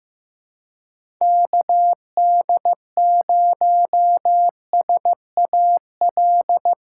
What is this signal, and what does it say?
Morse code, heard as a keyed tone, about 15 wpm, 700 Hz. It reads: KD0SAL
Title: USA Call Signs - 6 Characters - 4x Character Spacing - 15wpm